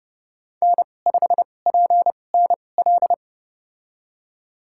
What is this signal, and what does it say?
Morse code, heard as a keyed tone, about 30 wpm, spaced as for 22 wpm, 700 Hz. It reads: N5PNL
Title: USA Call Signs - 5 Characters - 2x Character Spacing - 30wpm